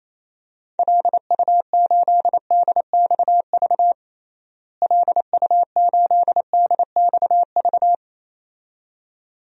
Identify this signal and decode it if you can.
Morse code, heard as a keyed tone, about 28 wpm, 700 Hz. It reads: LU8DXV LU8DXV
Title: DXCC-100 Call Signs - 6 Characters - Repeated 2x - 28wpm